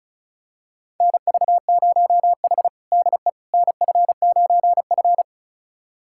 Morse code, heard as a keyed tone, about 36 wpm, 700 Hz. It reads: NV0H DE NF9F
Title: USA Callsign DE Callsign - 4 Characters - 35wpm